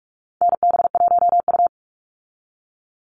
Morse code, heard as a keyed tone, about 44 wpm, 700 Hz. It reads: NB1U
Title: Call Signs Repeated 2x - 45wpm